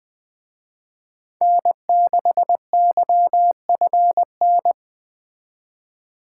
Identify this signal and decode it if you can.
Morse code, heard as a keyed tone, about 20 wpm, 700 Hz. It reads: N6YFN
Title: Speed Racing - USA 1x3 Call Signs 17wpm